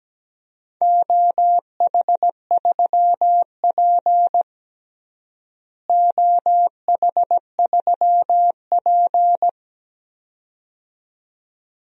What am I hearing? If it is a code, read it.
Morse code, heard as a keyed tone, about 17 wpm, 700 Hz. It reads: OH3P OH3P
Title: DXCC-100 Call Signs - 4 Characters - Repeated 2x - 17wpm